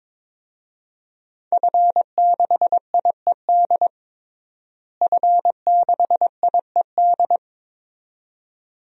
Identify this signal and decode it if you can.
Morse code, heard as a keyed tone, about 22 wpm, 700 Hz. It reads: F6IED F6IED
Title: DXCC-100 Call Signs - 5 Characters - Repeated 2x - 22wpm